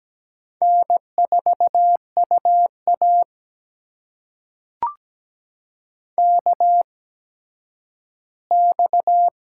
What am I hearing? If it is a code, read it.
Morse code, heard as a keyed tone, about 17 wpm, spaced as for 10 wpm, 700 Hz. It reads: N4UA KX
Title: USA Call Signs - 4 Characters - 8x Character Spacing - 17wpm